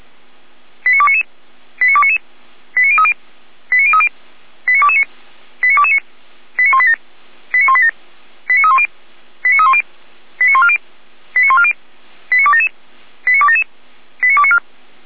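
Five-tone selective call (ZVEI), heard as quick runs of five tones, tones of about 70 ms. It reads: 89299 89299 89030 89030 89229 89229 89187 89187 89310 89310 89140 89140 89277 89277 89294
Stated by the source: ZVEI five-tone selective call received on an SDR, from the Feuerwehr Orlamünde fire brigade archive.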